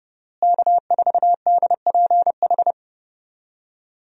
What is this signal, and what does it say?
Morse code, heard as a keyed tone, about 31 wpm, 700 Hz. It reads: K4DPH
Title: USA Call Signs - 5 Characters - Repeated 3x - 30wpm